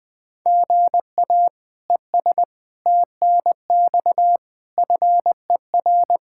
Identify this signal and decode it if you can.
Morse code, heard as a keyed tone, about 21 wpm, 700 Hz. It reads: GA ES TNX FER
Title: Practice QSO - Exchange 1 - Beginning - 20wpm